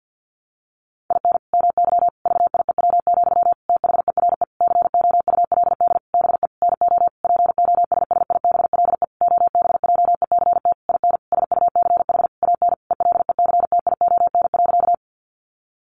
Morse code, heard as a keyed tone, about 50 wpm, 700 Hz. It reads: IN MY VIEW, THERE COULD BE NO POSSIBLE OBJECT IN SUCH AN ELECTION.
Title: Sentences from Top 1500 Words 50wpm